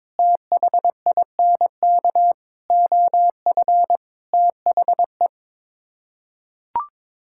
Morse code, heard as a keyed tone, about 22 wpm, 700 Hz. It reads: THINK OF THE
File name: practice / 22wpm / Sets of 6 Words - 22wpm